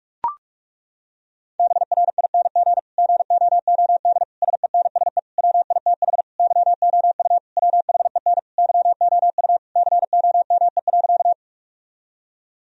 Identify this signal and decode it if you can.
Morse code, heard as a keyed tone, about 42 wpm, 700 Hz. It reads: BRING GOOD SENSE WITH YOU WHEN YOU COME.
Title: Sentences from Top 1000 Words - Review Part 4 - 45wpm